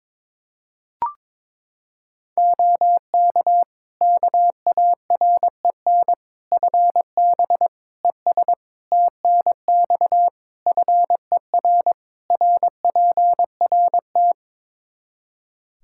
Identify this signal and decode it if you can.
Morse code, heard as a keyed tone, about 22 wpm, 700 Hz. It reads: OK KAREN FB ES TNX FER RPRT
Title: Practice QSO - Exchange 2 - Info - 22wpm